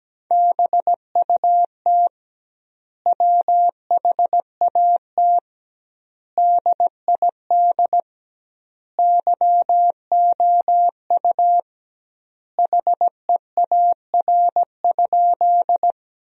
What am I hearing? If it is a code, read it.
Morse code, heard as a keyed tone, about 17 wpm, 700 Hz. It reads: BUT WHAT DID YOU HEAR?
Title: 2x Word-Spacing - Sentences from Top 300 Words - 17wpm